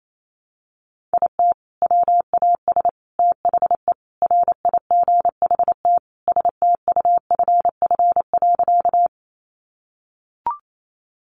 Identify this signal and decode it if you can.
Morse code, heard as a keyed tone, about 28 wpm, 700 Hz. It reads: IT WAS THE RIGHT STUFF.
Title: Sentences from Top 1400 Words 28wpm